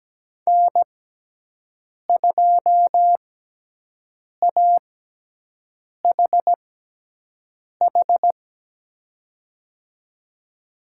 Morse code, heard as a keyed tone, about 17 wpm, spaced as for 6 wpm, 700 Hz. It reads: N2AHH
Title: USA Call Signs - 6x Character Spacing - 17wpm